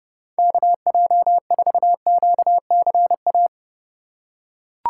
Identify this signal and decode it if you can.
Morse code, heard as a keyed tone, about 32 wpm, 700 Hz. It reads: KJ4QCA
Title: USA Call Signs - 4x Character Spacing - 30wpm